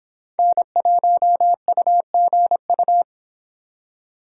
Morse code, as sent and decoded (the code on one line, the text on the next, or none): -. .---- ..- --. ..-
N1UGU